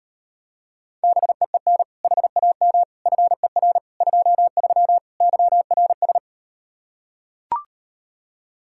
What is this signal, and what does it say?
Morse code, heard as a keyed tone, about 38 wpm, 700 Hz. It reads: BEEN HAM FER 23 YRS